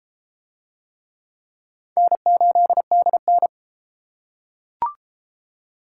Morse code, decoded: N8DN

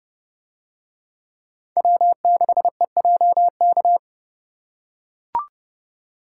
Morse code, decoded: W6EJK

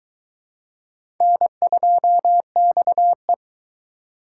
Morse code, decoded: N2XE

